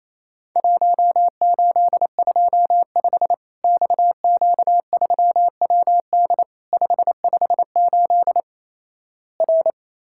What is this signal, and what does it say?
Morse code, heard as a keyed tone, about 28 wpm, 700 Hz. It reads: 1825 XQ3WD 558 R
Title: Worked All Europe DX Contest - QTC Exchange - 28wpm